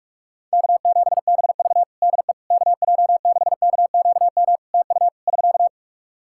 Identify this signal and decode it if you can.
Morse code, heard as a keyed tone, about 44 wpm, 700 Hz. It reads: K7BV DE KJ6KQM TU <SK>
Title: Practice QSO - Exchange 4 - Ending - 45wpm